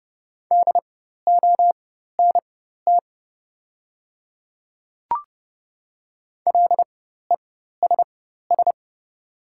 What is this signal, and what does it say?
Morse code, heard as a keyed tone, about 30 wpm, spaced as for 14 wpm, 700 Hz. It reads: DONT LESS
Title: 4x Character Spacing - Top 500 Words - 30wpm